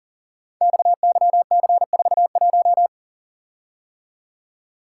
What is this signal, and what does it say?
Morse code, heard as a keyed tone, about 38 wpm, 690 Hz. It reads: XYC41
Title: Code Groups - 5 Letter:Numbers - 40wpm